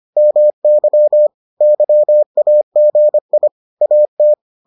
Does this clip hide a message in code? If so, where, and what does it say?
Yes: Morse code from 0.2 to 4.3 s, heard as a keyed tone, about 25 wpm, 600 Hz. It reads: MY YAGI AT